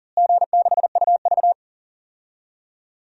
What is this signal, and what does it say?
Morse code, heard as a keyed tone, about 40 wpm, 700 Hz. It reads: G6UV
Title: DXCC-100 Call Signs - 4 Characters - Repeated 3x - 40wpm